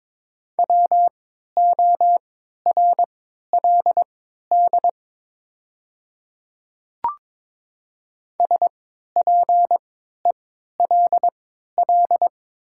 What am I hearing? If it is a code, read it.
Morse code, heard as a keyed tone, about 22 wpm, spaced as for 12 wpm, 700 Hz. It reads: WORLD SPELL
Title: 3x Character Spacing - Top 200 Words - 22wpm